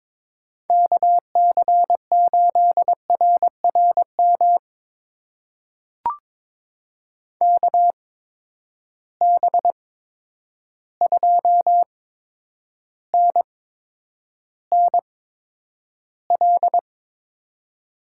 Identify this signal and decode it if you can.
Morse code, heard as a keyed tone, about 22 wpm, spaced as for 9 wpm, 700 Hz. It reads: KC8RRM KB2NNL
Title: USA Call Signs - 6 Characters - 8x Character Spacing - 22wpm